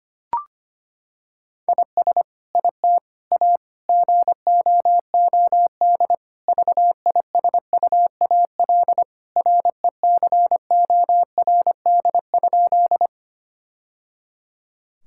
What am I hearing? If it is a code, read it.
Morse code, heard as a keyed tone, about 25 wpm, 700 Hz. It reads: IS IT A GOOD VISUAL RECORD?
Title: Sentences from Top 1000 Words - Review Part 6 - 25wpm